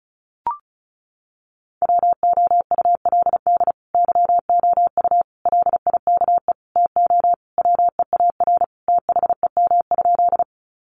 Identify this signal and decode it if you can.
Morse code, heard as a keyed tone, about 34 wpm, 700 Hz. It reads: WOULD YOU LIKE TO WEAR THEM?
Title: Sentences from Top 1500 Words 35wpm